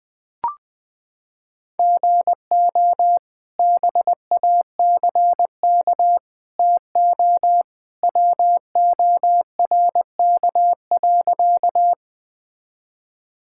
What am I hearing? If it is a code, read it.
Morse code, heard as a keyed tone, about 20 wpm, 700 Hz. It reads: GO BACK TO WORK.
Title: Sentences with 2-Word Phrases - 20wpm